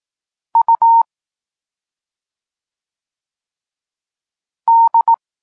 Morse code, heard as a keyed tone, about 18 wpm, 920 Hz. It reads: U D